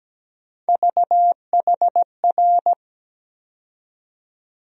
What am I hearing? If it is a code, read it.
Morse code, heard as a keyed tone, about 17 wpm, 700 Hz. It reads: VHR
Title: USA Call Sign Postfixes - 3 Letters 17wpm